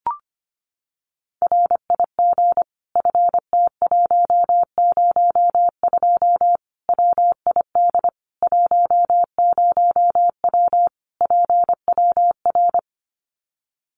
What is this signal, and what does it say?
Morse code, heard as a keyed tone, about 25 wpm, 700 Hz. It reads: RIG FT102 WID 10W PWR
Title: Common QSO Phrases 25wpm